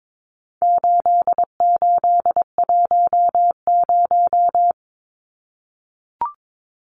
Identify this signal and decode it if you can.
Morse code, heard as a keyed tone, about 22 wpm, 700 Hz. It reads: 8810
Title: Numbers - 4-Digit 22wpm